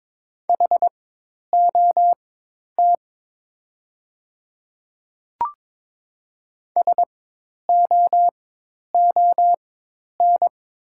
Morse code, heard as a keyed tone, about 22 wpm, spaced as for 10 wpm, 700 Hz. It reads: HOT SOON
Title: 4x Character Spacing - Top 500 Words - 22wpm